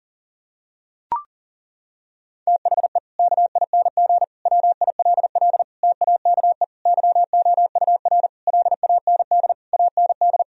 Morse code, heard as a keyed tone, about 37 wpm, 700 Hz. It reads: THE KING WILL TAKE YOUR LAND AND
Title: Sentences from Top 1000 Words - Review Part 4 - 40wpm